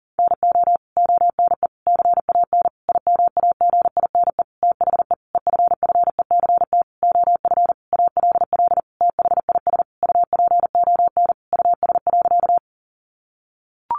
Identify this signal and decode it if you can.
Morse code, heard as a keyed tone, about 40 wpm, 700 Hz. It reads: NO ONE CAN IMAGINE THE EFFECT OF ALL THIS UPON US.
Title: Sentences from Top 1400 Words 40wpm